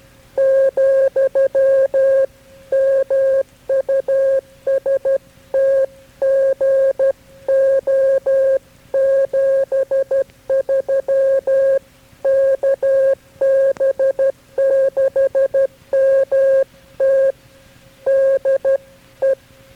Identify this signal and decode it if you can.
Morse code, heard as a keyed tone, about 12 wpm, 540 Hz. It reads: , MUST GO 73 KB6MT DE
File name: KB6MT-tape3-side1